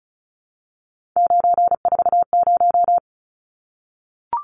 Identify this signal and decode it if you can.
Morse code, heard as a keyed tone, about 34 wpm, 700 Hz. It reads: 940
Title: Numbers - 3-Digit 35wpm